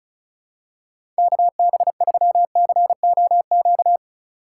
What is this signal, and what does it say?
Morse code, heard as a keyed tone, about 34 wpm, 700 Hz. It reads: KB3COQ
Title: USA Call Signs - 6 Characters - 3x Character Spacing - 35wpm